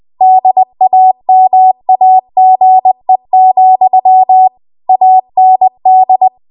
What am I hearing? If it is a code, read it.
Morse code, heard as a keyed tone, about 20 wpm, 750 Hz. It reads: DAMAGE, AND